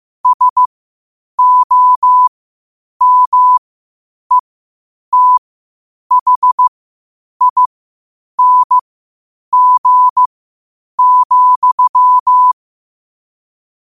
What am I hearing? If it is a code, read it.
Morse code, heard as a keyed tone, about 15 wpm, spaced as for 8 wpm, 1000 Hz. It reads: SOMETHING,
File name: qotdfiles / 8wpm